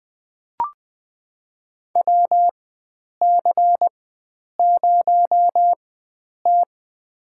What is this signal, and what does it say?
Morse code, heard as a keyed tone, about 20 wpm, spaced as for 9 wpm, 700 Hz. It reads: WC0T
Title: USA Call Signs - 4 Characters - 4x Character Spacing - 20wpm